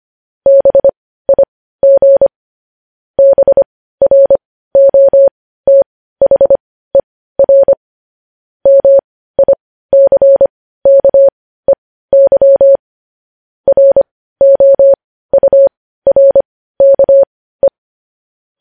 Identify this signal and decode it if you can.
Morse code, heard as a keyed tone, about 25 wpm, spaced as for 15 wpm, 550 Hz. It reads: BIG BROTHER MICKEY ROURKE
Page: News Headlines in Morse Code at 15 WPM